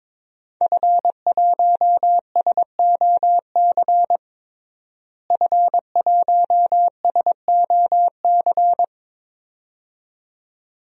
Morse code, heard as a keyed tone, about 21 wpm, 700 Hz. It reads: F1SOC F1SOC
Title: DXCC-100 Call Signs - 5 Characters - Repeated 2x - 22wpm